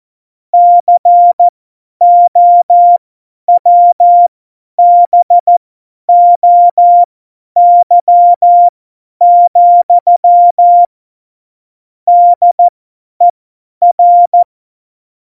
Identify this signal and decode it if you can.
Morse code, heard as a keyed tone, about 14 wpm, spaced as for 10 wpm, 700 Hz. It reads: COWBOY, DER
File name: Morsekurs / Witze / Witz 11. Juni 2019, 50 F70